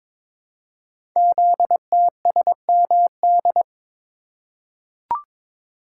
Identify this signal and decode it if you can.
Morse code, heard as a keyed tone, about 22 wpm, 700 Hz. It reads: ZTSMD
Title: Code Groups - 5 Letters - 22wpm